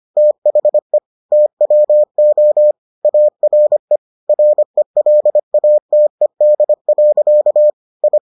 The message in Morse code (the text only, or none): THE TWO ARE RELATED. I